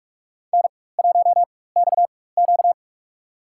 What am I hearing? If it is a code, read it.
Morse code, heard as a keyed tone, about 46 wpm, spaced as for 21 wpm, 700 Hz. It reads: N1XQ
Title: USA Call Signs - 4 Characters - 4x Character Spacing - 45wpm